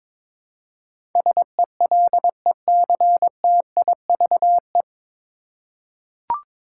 Morse code, heard as a keyed tone, about 22 wpm, 700 Hz. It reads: SELECTIVE
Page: Morse Code Ninja practice 22wpm